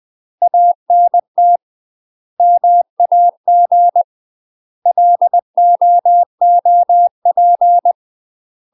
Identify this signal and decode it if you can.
Morse code, heard as a keyed tone, about 20 wpm, 700 Hz. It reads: ANT MAG LOOP
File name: UK MASTER QSOs TRAINING / 20WPM / UK MASTER QSOs 0025